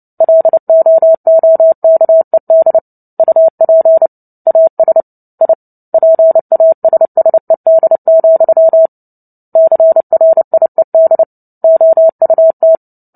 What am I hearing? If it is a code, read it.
Morse code, heard as a keyed tone, about 30 wpm, 650 Hz. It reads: LOOKED UP AS I PASSED, CRIED OUT
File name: Book Two / 03-Chapter 3 - 29 WPM